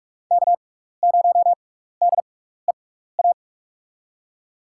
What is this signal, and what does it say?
Morse code, heard as a keyed tone, about 44 wpm, spaced as for 15 wpm, 700 Hz. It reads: K0DEA
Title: USA Call Signs - 5 Characters - 6x Character Spacing - 45wpm